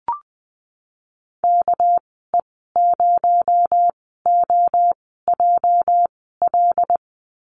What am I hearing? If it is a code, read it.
Morse code, heard as a keyed tone, about 20 wpm, spaced as for 14 wpm, 700 Hz. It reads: KE0OJL
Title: 245 Lesson 39g - Letter z - Call Signs 20wpm